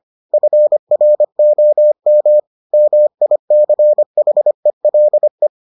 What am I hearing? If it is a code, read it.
Morse code, heard as a keyed tone, about 25 wpm, 600 Hz. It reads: FROM MICHELE